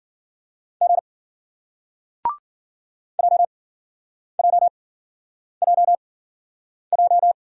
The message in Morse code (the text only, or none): G JJJJ